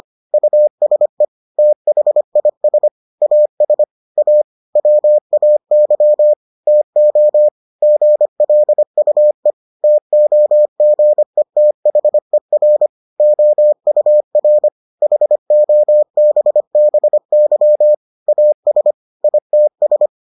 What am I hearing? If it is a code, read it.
Morse code, heard as a keyed tone, about 25 wpm, 600 Hz. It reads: USE THIS AS A WAY TO GLUE TOGETHER OUR HOBBY AS ITS